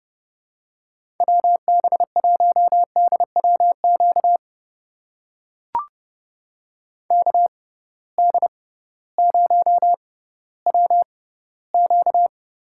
Morse code, heard as a keyed tone, about 32 wpm, spaced as for 17 wpm, 700 Hz. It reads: WB1DWQ KD0WQ
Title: USA Call Signs - 6 Characters - 6x Character Spacing - 30wpm